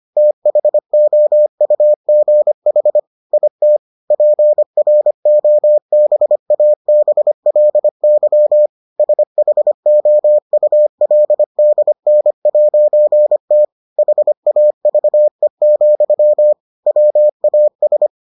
Morse code, heard as a keyed tone, about 25 wpm, 600 Hz. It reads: THOUGH IT PROBABLY SHOULDN'T HAVE, WAS